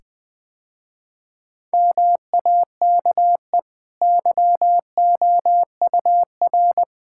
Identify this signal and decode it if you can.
Morse code, heard as a keyed tone, about 20 wpm, 700 Hz. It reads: MAKE YOUR
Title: Easy Sentences with Hints - 20wpm